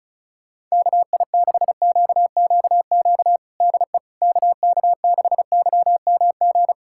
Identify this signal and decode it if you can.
Morse code, heard as a keyed tone, about 34 wpm, 700 Hz. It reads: KI6QQQ DE KK6YMG
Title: USA Callsign DE Callsign 35wpm